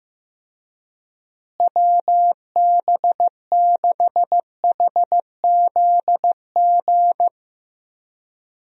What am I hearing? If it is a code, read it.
Morse code, heard as a keyed tone, about 16 wpm, 700 Hz. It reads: WB6HZG
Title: USA Call Signs - 6 Characters - Repeated 3x - 15wpm